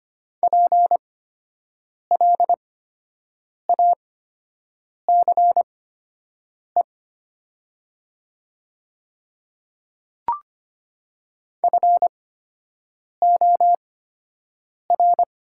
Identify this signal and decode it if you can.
Morse code, heard as a keyed tone, about 25 wpm, spaced as for 7 wpm, 700 Hz. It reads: PLACE FOR
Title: 8x Character Spacing - Top 200 Words - 25wpm